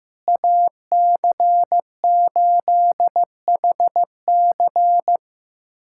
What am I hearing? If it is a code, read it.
Morse code, heard as a keyed tone, about 15 wpm, 700 Hz. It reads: AC8HC
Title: USA Callsign DE Callsign 15wpm